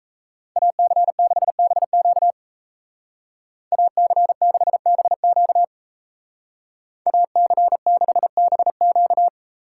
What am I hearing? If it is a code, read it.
Morse code, heard as a keyed tone, about 37 wpm, 700 Hz. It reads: AC6BQ AC6BQ AC6BQ